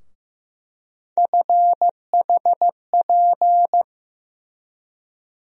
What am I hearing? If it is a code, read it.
Morse code, heard as a keyed tone, about 15 wpm, 700 Hz. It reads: FHP
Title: USA Call Sign Postfixes - 3 Letters 15wpm